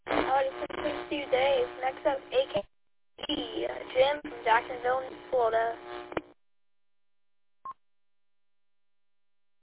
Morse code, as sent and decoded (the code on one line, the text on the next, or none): none